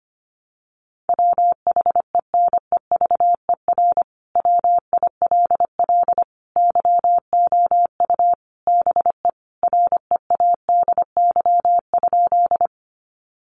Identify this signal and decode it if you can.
Morse code, heard as a keyed tone, about 25 wpm, 700 Hz. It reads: WHENEVER WILL YOU BE READY?
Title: Sentences from Top 1200 Words 25wpm